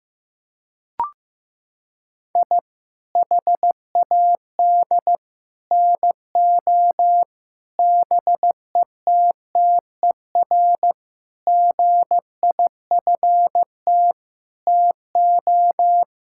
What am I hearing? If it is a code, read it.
Morse code, heard as a keyed tone, about 14 wpm, 700 Hz. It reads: I HAD NO BETTER GIFT TO
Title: Sentences from Top 1000 Words - Review Part 6 - 15wpm